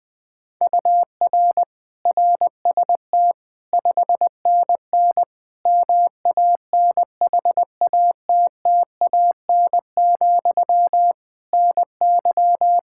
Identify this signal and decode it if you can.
Morse code, heard as a keyed tone, about 20 wpm, 700 Hz. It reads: UR RST 5NN MANHATTAN, NY
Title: Practice QSO - Exchange 1 - Beginning - 20wpm